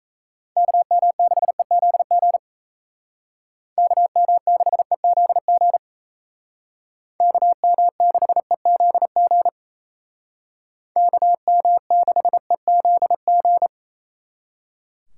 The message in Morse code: KM6EZG KM6EZG KM6EZG KM6EZG

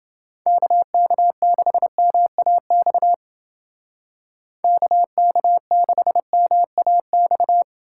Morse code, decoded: KK6MAX KK6MAX

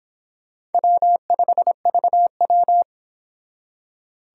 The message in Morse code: W5VW

